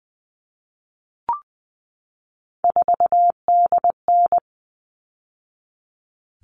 Morse code, heard as a keyed tone, about 20 wpm, 700 Hz. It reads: 4DN
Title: All Cut Numbers - 3-Digit 20wpm